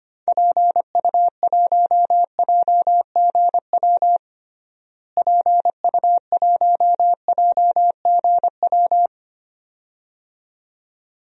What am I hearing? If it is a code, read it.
Morse code, heard as a keyed tone, about 25 wpm, 700 Hz. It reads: PU1JGW PU1JGW
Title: DXCC-100 Call Signs - 6 Characters - Repeated 2x - 25wpm